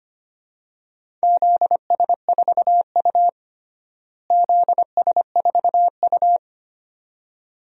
Morse code, heard as a keyed tone, about 25 wpm, 700 Hz. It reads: ZS4U ZS4U
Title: DXCC-100 Call Signs - 4 Characters - Repeated 2x - 25wpm